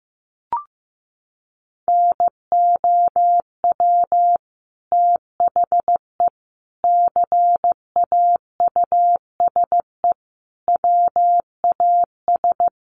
Morse code, heard as a keyed tone, about 15 wpm, 700 Hz. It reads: NOW THE CAUSE WAS
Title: Sentences from Top 1500 Words 15wpm